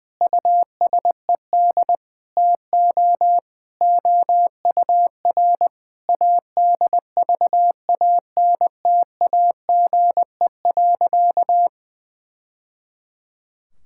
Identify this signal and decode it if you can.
Morse code, heard as a keyed tone, about 20 wpm, 700 Hz. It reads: USED TO OUR ADVANTAGE.
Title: Sentences from Top 1000 Words - Review Part 6 - 20wpm